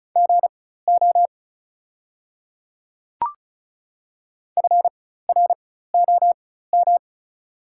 Morse code, heard as a keyed tone, about 36 wpm, spaced as for 16 wpm, 690 Hz. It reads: GO FROM
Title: 4x Character Spacing - Top 200 Words - 35wpm